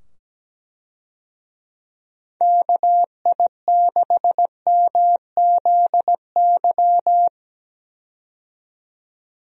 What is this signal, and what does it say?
Morse code, heard as a keyed tone, about 17 wpm, 700 Hz. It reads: KI6MZY
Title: USA Call Signs - 6 Characters - 17wpm